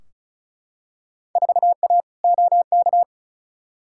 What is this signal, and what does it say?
Morse code, heard as a keyed tone, about 36 wpm, 700 Hz. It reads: 4A OK
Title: Field Day Exchanges - 35wpm